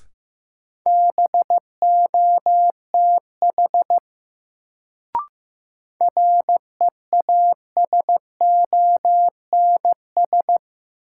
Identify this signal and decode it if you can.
Morse code, heard as a keyed tone, about 15 wpm, 700 Hz. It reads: BOTH REASONS
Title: Sets of 4 Words - New Word Drill - 15wpm